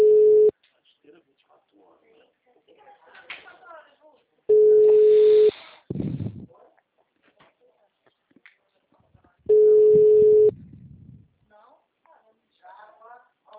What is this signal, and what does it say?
Ringback tone (Europe, 425 Hz), with speech.